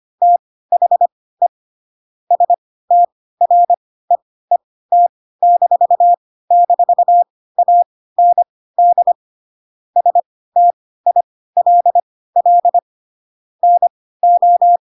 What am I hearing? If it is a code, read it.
Morse code, heard as a keyed tone, about 25 wpm, spaced as for 16 wpm, 700 Hz. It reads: THE STREET--AND STILL NO
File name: Pinocchio / Pinocchio ch1 10 25 ik1hge